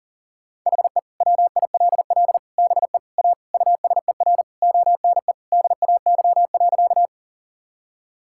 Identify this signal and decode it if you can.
Morse code, heard as a keyed tone, about 40 wpm, 700 Hz. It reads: HE WILL BE A USER ONE DAY.
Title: Sentences from Top 1000 Words - Review Part 5 - 40wpm